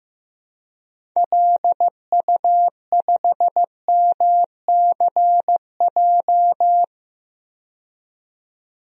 Morse code, heard as a keyed tone, about 15 wpm, 700 Hz. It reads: LU5MCJ
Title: DXCC-100 Call Signs - 6 Characters - 15wpm